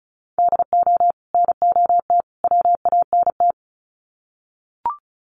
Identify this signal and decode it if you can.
Morse code, heard as a keyed tone, about 34 wpm, 700 Hz. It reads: DO NOT WANT
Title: Sets of 3 Words 35wpm